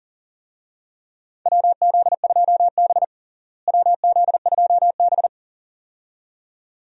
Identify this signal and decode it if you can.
Morse code, heard as a keyed tone, about 40 wpm, 700 Hz. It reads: WZ2B WZ2B
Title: USA Call Signs - 4 Characters - Repeated 2x - 40wpm